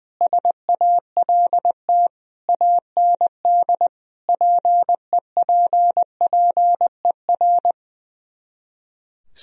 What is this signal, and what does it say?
Morse code, heard as a keyed tone, about 19 wpm, 700 Hz. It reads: SALT AND PEPPER
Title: Speed-Racing - Sets of 5 Words 20wpm